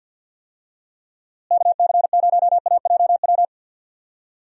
Morse code, heard as a keyed tone, about 52 wpm, 700 Hz. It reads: KC0AJW
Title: USA Call Signs - 6 Characters - Repeated 3x - 50wpm